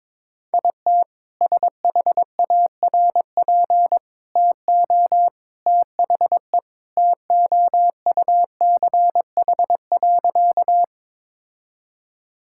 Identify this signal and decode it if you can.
Morse code, heard as a keyed tone, about 22 wpm, 700 Hz. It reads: IT SHARP TO THE TOUCH.
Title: Sentences from Top 1600 Words 22wpm